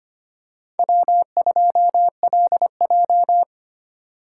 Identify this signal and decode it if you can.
Morse code, heard as a keyed tone, about 25 wpm, 700 Hz. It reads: W2LJ
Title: USA Call Signs - 4x Character Spacing - 25wpm